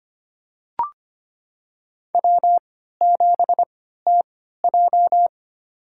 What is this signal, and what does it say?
Morse code, heard as a keyed tone, about 25 wpm, spaced as for 14 wpm, 700 Hz. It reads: W7TJ
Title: USA Call Signs - 4 Characters - 3x Character Spacing - 25wpm